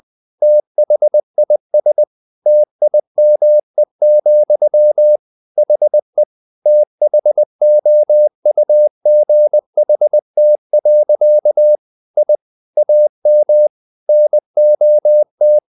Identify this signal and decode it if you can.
Morse code, heard as a keyed tone, about 20 wpm, 600 Hz. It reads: THIS TIME, HE THOUGHT. I AM NOT
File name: the old man and the sea part 2 20wpm